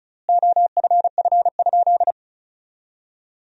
Morse code, heard as a keyed tone, about 34 wpm, 700 Hz. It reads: OFF?
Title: 4x Word-Spacing - Sentences from Top 500 Words - 35wpm